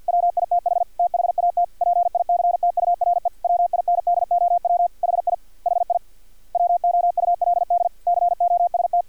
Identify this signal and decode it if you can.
Morse code, heard as a keyed tone, about 50 wpm, 700 Hz, with radio noise. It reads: WITH THAT PICTURE WINDOW HI HI WOULD COST
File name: archived / CRSnet